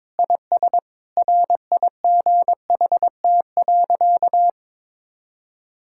Morse code, heard as a keyed tone, about 21 wpm, 700 Hz. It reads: IS RIGHT.
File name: practice / 22wpm / Sentences with 3-Word Phrases - 22wpm